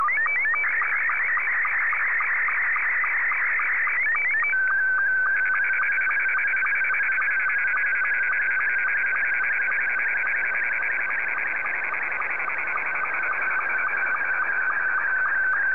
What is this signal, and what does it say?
SSTV (slow-scan television), Scottie 2.